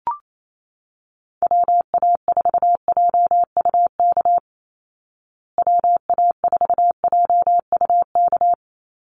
Morse code, heard as a keyed tone, about 28 wpm, 700 Hz. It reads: WA4JUK WA4JUK
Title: Call Signs Repeated 3x - 28wpm